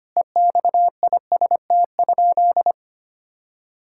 Morse code, heard as a keyed tone, about 25 wpm, 700 Hz. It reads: EXIST?
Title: Sentences from Top 1000 Words - Review Part 5 - 25wpm